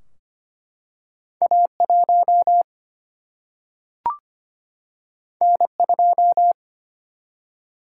Morse code, heard as a keyed tone, about 25 wpm, 700 Hz. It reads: A 1 N 2
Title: Single-Letter Single-Number 25wpm